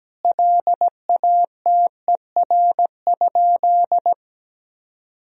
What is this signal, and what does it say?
Morse code, heard as a keyed tone, about 17 wpm, 700 Hz. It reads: LATER?
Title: Sentences from Top 1000 Words - Review Part 4 - 17wpm